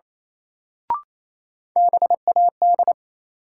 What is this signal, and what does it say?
Morse code, heard as a keyed tone, about 28 wpm, 700 Hz. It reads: BAD
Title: Sentences from Top 700 Words - New Word Drill - 28wpm